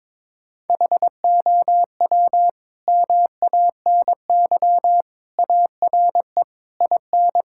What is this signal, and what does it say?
Morse code, heard as a keyed tone, about 22 wpm, 700 Hz. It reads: HOW MANY ARE IN
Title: Sentences from Top 1000 Words - Review Part 5 - 22wpm